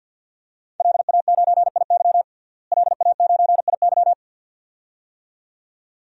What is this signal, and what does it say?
Morse code, heard as a keyed tone, about 51 wpm, 700 Hz. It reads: LA9IY LA9IY
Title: DXCC-100 Call Signs - 5 Characters - Repeated 2x - 50wpm